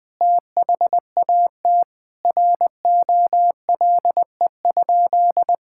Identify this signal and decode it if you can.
Morse code, heard as a keyed tone, about 21 wpm, 700 Hz. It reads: THAT ROLE?
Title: Sentences from Top 1000 Words - Review Part 6 - 20wpm